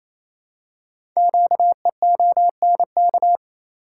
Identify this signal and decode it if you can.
Morse code, heard as a keyed tone, about 29 wpm, 700 Hz. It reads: QEONK
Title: Code Groups - 5 Letters - 28wpm